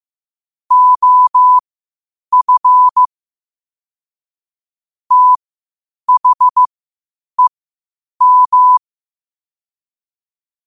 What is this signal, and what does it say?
Morse code, heard as a keyed tone, about 15 wpm, spaced as for 8 wpm, 1000 Hz. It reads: OF THEM
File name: qotdfiles / 8wpm